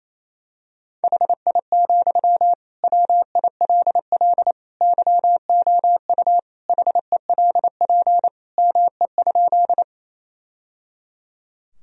Morse code, heard as a keyed tone, about 27 wpm, 700 Hz. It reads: HI, WILL YOU HELP ME?
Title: Sentences from Top 1000 Words - Review Part 3 - 28wpm